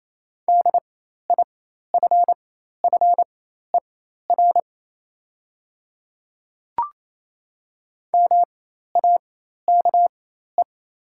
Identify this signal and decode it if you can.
Morse code, heard as a keyed tone, about 28 wpm, spaced as for 13 wpm, 690 Hz. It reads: DIFFER MAKE
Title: 4x Character Spacing - Top 200 Words - 28wpm